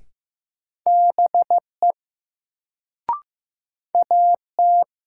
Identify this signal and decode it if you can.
Morse code, heard as a keyed tone, about 15 wpm, 700 Hz. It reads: BE AT
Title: Sentences from Top 100 Words - New Word Drill - 15wpm